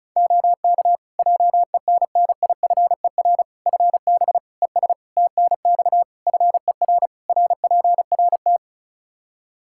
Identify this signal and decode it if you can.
Morse code, heard as a keyed tone, about 36 wpm, 700 Hz. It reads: OK JENNIFER FB ES TNX FER RPRT
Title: Practice QSO - Exchange 2 - Info - 35wpm